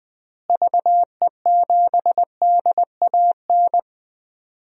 Morse code, heard as a keyed tone, about 20 wpm, 700 Hz. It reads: VE7DAN
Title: DXCC-100 Call Signs - 6 Characters - Repeated 2x - 20wpm